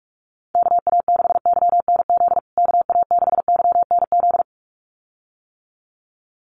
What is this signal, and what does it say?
Morse code, heard as a keyed tone, about 44 wpm, 700 Hz. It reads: KA6YNZ KA6YNZ
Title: USA Call Signs Repeated 2x 45wpm